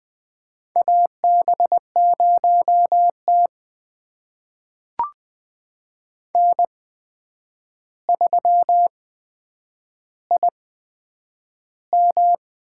Morse code, heard as a keyed tone, about 21 wpm, spaced as for 9 wpm, 700 Hz. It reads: AB0T N3IM
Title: USA Call Signs - 4 Characters - 8x Character Spacing - 20wpm